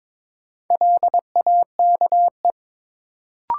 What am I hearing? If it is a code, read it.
Morse code, heard as a keyed tone, about 22 wpm, 700 Hz. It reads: LAKE
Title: Sentences from Top 1000 Words - Review Part 3 - New Word Drill - 22wpm